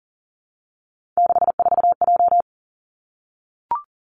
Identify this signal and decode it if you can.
Morse code, heard as a keyed tone, about 38 wpm, 700 Hz. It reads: 64J
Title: Three Challenging Letter-Number Combinations 40wpm